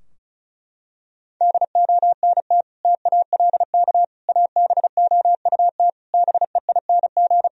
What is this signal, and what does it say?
Morse code, heard as a keyed tone, about 36 wpm, 700 Hz. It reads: DONT TALK ABOUT BEING
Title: Sentences from Top 1000 Words - Review Part 4 - 35wpm